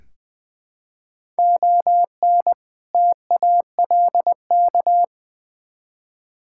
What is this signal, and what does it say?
Morse code, heard as a keyed tone, about 20 wpm, 690 Hz. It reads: ON TALK
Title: Build A Phrase - 5-Word Phrases - 20wpm